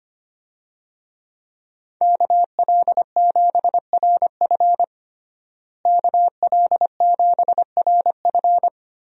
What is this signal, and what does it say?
Morse code, heard as a keyed tone, about 25 wpm, 700 Hz. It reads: KL7RF KL7RF
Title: DXCC-100 Call Signs - 5 Characters - Repeated 2x - 25wpm